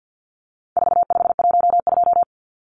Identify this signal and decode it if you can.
Morse code, heard as a keyed tone, about 50 wpm, 700 Hz. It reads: 4512
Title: Numbers - 4-Digit 50wpm